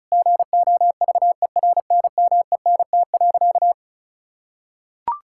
Morse code, heard as a keyed tone, about 36 wpm, 700 Hz. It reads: GOVERNMENT.
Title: Sentences from Top 1000 Words - Review Part 6 - 35wpm